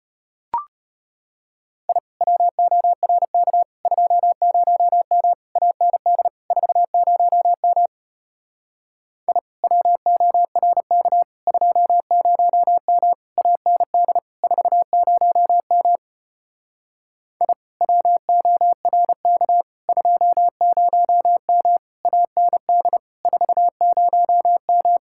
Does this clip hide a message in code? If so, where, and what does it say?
Yes: Morse code from 1.9 to 25.0 s, heard as a keyed tone, about 33 wpm, 700 Hz. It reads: I WORK 20M AND 40M I WORK 20M AND 40M I WORK 20M AND 40M